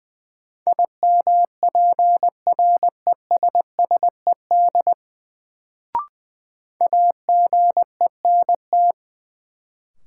Morse code, heard as a keyed tone, about 20 wpm, 700 Hz. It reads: IMPRESSED AGENT
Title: Sentences from Top 1600 Words - New Word Drill - 20wpm